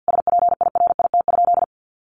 Morse code, heard as a keyed tone, about 50 wpm, 700 Hz. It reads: SPIRIT?
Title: Sentences from Top 1400 Words 50wpm